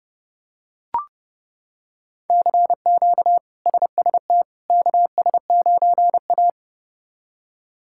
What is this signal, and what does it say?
Morse code, heard as a keyed tone, about 30 wpm, 700 Hz. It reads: CQ SST KS9A
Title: K1USN SST QSO Practice - 30wpm